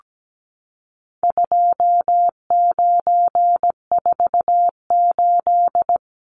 Morse code, heard as a keyed tone, about 17 wpm, 700 Hz. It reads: 2948
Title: Numbers - 4-Digit 17wpm